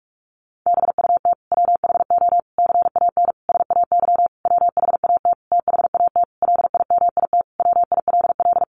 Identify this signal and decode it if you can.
Morse code, heard as a keyed tone, about 44 wpm, 700 Hz. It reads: BUT WHO CAN SAY WHAT THAT LIMIT WILL
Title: Sentences from Top 1500 Words 45wpm